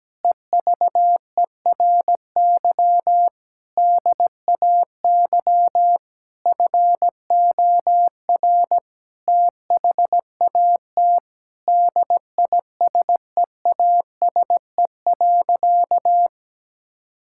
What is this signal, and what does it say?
Morse code, heard as a keyed tone, about 17 wpm, 700 Hz. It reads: EVERY DAY FOR THAT DISEASE.
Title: Sentences from Top 1000 Words - Review Part 6 - 17wpm